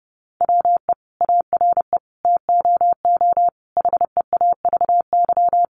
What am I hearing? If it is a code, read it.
Morse code, heard as a keyed tone, about 30 wpm, 700 Hz. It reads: WE ARE TOO HEAVY
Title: Sentences from Top 1100 Words 30wpm